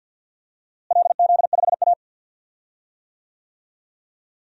Morse code, heard as a keyed tone, about 49 wpm, 700 Hz. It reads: LZ5A
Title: DXCC-100 Call Signs - 4 Characters - Repeated 3x - 50wpm